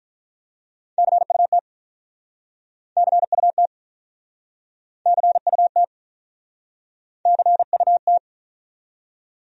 Morse code, heard as a keyed tone, about 48 wpm, 700 Hz. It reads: CUT CUT CUT CUT